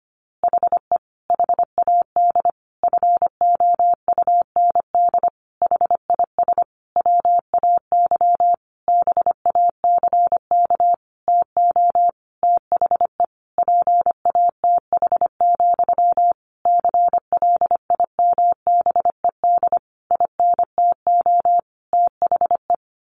Morse code, heard as a keyed tone, about 25 wpm, 700 Hz. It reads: HE HAD FOUND HIS WAY BACK TO THE PATH, CLIMBED INTO THE
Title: A Sound Of Thunder 25wpm